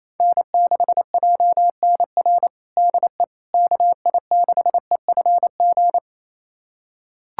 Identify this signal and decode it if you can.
Morse code, heard as a keyed tone, about 28 wpm, 700 Hz. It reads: N6JNR DE KI6EFG
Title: USA Callsign DE Callsign 28wpm